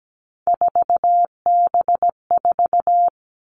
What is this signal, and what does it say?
Morse code, heard as a keyed tone, about 17 wpm, 700 Hz. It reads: 4B4
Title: Three Challenging Letter-Number Combinations 17wpm